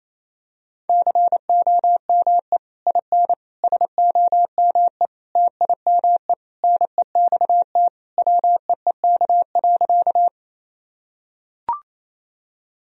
Morse code, heard as a keyed tone, about 27 wpm, 700 Hz. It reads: COME IN SOME TIME NEXT WEEK.
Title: Sentences from Top 1000 Words - Review Part 2 - 28wpm